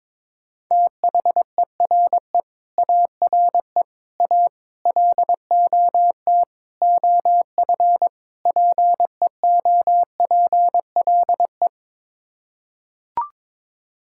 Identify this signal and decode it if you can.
Morse code, heard as a keyed tone, about 22 wpm, 700 Hz. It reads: THERE ARE A LOT OF PEOPLE
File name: practice / 22wpm / Sets of 6 Words - 22wpm